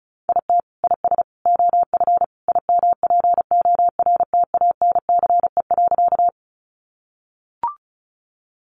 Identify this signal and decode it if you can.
Morse code, heard as a keyed tone, about 34 wpm, 700 Hz. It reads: IT IS OF IMPORTANCE.
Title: Sentences from Top 1200 Words 35wpm